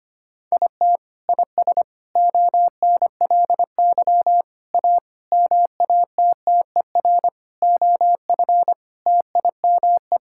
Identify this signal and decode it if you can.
Morse code, heard as a keyed tone, about 25 wpm, 700 Hz. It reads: IT IS ONLY A MATTER OF TIME
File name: Sets of 9 Words - 25wpm